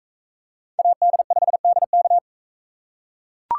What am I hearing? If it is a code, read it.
Morse code, heard as a keyed tone, about 37 wpm, 700 Hz. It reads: AD5DK